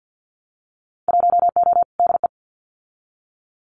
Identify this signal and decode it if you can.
Morse code, heard as a keyed tone, about 50 wpm, 700 Hz. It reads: 1O DE
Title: Winter Field Day Exchanges 50wpm